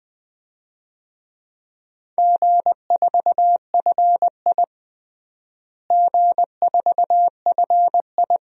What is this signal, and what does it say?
Morse code, heard as a keyed tone, about 20 wpm, 700 Hz. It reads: G4FI G4FI
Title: DXCC-100 Call Signs - 4 Characters - Repeated 3x - 20wpm